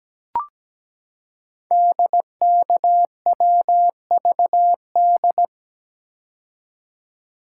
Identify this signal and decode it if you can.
Morse code, heard as a keyed tone, about 17 wpm, 700 Hz. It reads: DKWVD
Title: Code Groups - 5 Letters - 17wpm